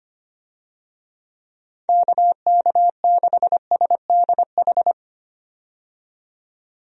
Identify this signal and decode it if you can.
Morse code, heard as a keyed tone, about 25 wpm, 700 Hz. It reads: KK6SDH
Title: USA Call Signs - 6 Characters - 25wpm